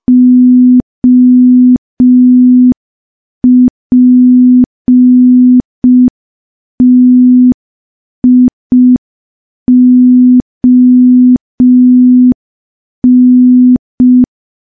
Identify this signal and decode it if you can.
Morse code, heard as a keyed tone, about 5 wpm, 250 Hz. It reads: OPTION
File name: morse / morse-training-05WPM